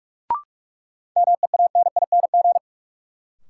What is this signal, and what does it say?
Morse code, heard as a keyed tone, about 43 wpm, 700 Hz. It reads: MEANING
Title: Sentences from Top 900 Words - New Word Drill - 45wpm